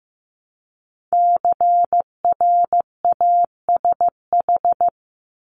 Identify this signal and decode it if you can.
Morse code, heard as a keyed tone, about 15 wpm, 700 Hz. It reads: CRASH
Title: Multi-letter Phonemes Repeated 3x 15wpm